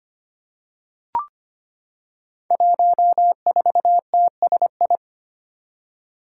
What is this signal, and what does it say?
Morse code, heard as a keyed tone, about 25 wpm, 700 Hz. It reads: 14TSI
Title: Five Challenging Letter-Number Combinations 25wpm